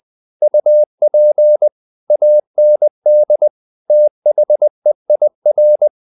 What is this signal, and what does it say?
Morse code, heard as a keyed tone, about 20 wpm, 600 Hz. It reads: UP AND THEIR
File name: the old man and the sea / the old man and the sea part 1 20wpm